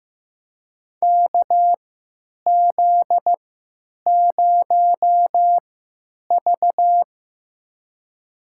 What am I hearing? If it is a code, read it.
Morse code, heard as a keyed tone, about 15 wpm, spaced as for 8 wpm, 700 Hz. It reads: KZ0V
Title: USA Call Signs - 4 Characters - 3x Character Spacing - 15wpm